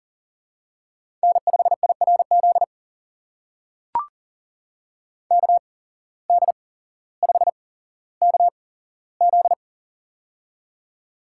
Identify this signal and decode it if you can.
Morse code, heard as a keyed tone, about 40 wpm, spaced as for 17 wpm, 700 Hz. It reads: N5IRZ KD5KZ
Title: USA Call Signs - 5 Characters - 8x Character Spacing - 40wpm